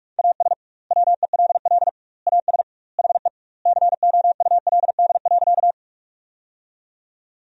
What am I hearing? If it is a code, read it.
Morse code, heard as a keyed tone, about 46 wpm, 700 Hz. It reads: AS WELL AS HE COULD.
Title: 2x Word-Spacing - Sentences from Top 500 Words - 45wpm